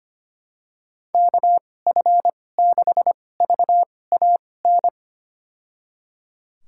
Morse code, heard as a keyed tone, about 25 wpm, spaced as for 18 wpm, 700 Hz. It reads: KF6VAN
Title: USA Call Signs - 6 Characters - 2x Character Spacing - 25wpm